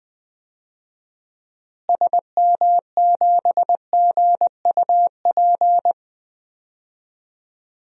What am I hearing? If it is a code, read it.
Morse code, heard as a keyed tone, about 20 wpm, 700 Hz. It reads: SM7GUP
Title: DXCC-100 Call Signs - 6 Characters - Repeated 2x - 20wpm